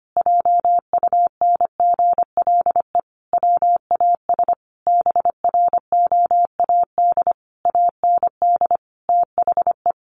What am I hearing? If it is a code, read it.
Morse code, heard as a keyed tone, about 25 wpm, 700 Hz. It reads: JUNGLE WAS BROAD AND THE
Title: A Sound Of Thunder 25wpm